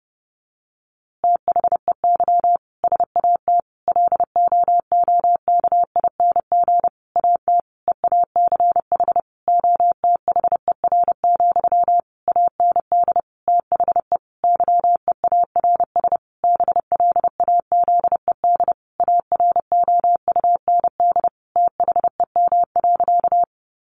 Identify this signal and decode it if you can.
Morse code, heard as a keyed tone, about 30 wpm, 700 Hz. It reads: THEY SAT LOOKING AT EACH OTHER, AND THE YEARS BLAZED AROUND THEM.
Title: A Sound Of Thunder 30wpm